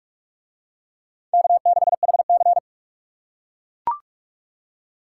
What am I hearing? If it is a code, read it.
Morse code, heard as a keyed tone, about 42 wpm, 700 Hz. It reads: K6HC